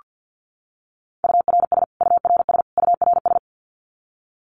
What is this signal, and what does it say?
Morse code, heard as a keyed tone, about 48 wpm, 700 Hz. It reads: URS URS URS
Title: Abbreviations Repeated 3x - 50wpm